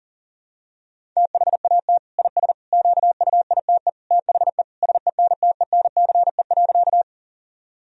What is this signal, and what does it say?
Morse code, heard as a keyed tone, about 37 wpm, 700 Hz. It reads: THAT IS QUITE THE SENTENCE.
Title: Sentences from Top 1000 Words - Review Part 1 - 40wpm